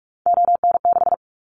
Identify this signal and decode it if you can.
Morse code, heard as a keyed tone, about 44 wpm, 700 Hz. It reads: ON6
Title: Callsign Prefixes for Top 100 DXCC - 45wpm